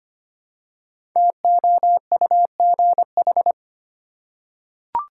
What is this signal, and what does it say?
Morse code, heard as a keyed tone, about 25 wpm, 700 Hz. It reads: TOUGH
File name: 5 Letter Words - First 3 Letters Given - 25wpm